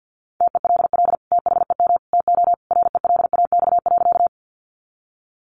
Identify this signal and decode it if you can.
Morse code, heard as a keyed tone, about 50 wpm, 700 Hz. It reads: TELL THEM TO RELAX.
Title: Sentences from Top 1400 Words 50wpm